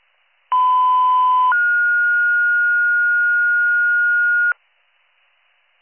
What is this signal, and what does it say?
Two-tone page, heard as a one-second tone, then a three-second tone, about 1010 then 1430 Hz, with noise. It is heard from 0.5 to 4.5 s.